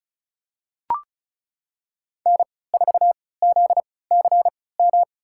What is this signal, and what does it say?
Morse code, heard as a keyed tone, about 36 wpm, spaced as for 20 wpm, 700 Hz. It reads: N4ZCM
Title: USA Call Signs - 5 Characters - 3x Character Spacing - 35wpm